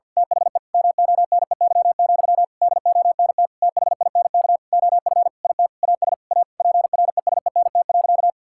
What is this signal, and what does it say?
Morse code, heard as a keyed tone, about 48 wpm, 700 Hz. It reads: THE MONEY, DONT THINK OF IT AS A PRESENT.
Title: Sentences from Top 1000 Words - Review Part 5 - 50wpm